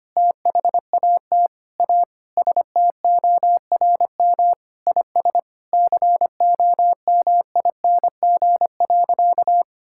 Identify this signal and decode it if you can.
Morse code, heard as a keyed tone, about 25 wpm, 700 Hz. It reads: THAT A STORM IS COMING.
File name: practice / 25wpm / Sentences with 2-Word Phrases - 25wpm